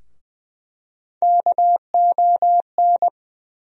Morse code, heard as a keyed tone, about 21 wpm, 700 Hz. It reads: KON
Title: USA Call Sign Postfixes - 3 Letters 20wpm